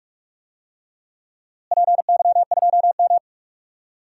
Morse code, heard as a keyed tone, about 46 wpm, 700 Hz. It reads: PY2M
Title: DXCC-100 Call Signs - 4 Characters - Repeated 2x - 45wpm